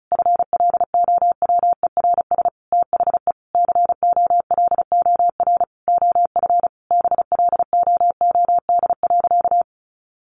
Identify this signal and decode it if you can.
Morse code, heard as a keyed tone, about 34 wpm, 700 Hz. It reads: FLOWERS THE COLOR OF BLOOD.
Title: A Sound Of Thunder 35wpm